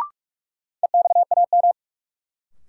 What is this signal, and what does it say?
Morse code, heard as a keyed tone, about 42 wpm, 700 Hz. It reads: EXAM
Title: Sentences from Top 1000 Words - Review Part 5 - New Word Drill - 45wpm